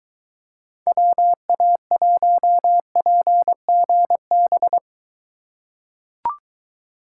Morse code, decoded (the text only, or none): WA1PGB